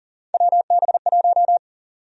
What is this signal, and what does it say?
Morse code, heard as a keyed tone, about 40 wpm, 700 Hz. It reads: WB1
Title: USA Call Sign Prefixes - Repeated 2x - 40wpm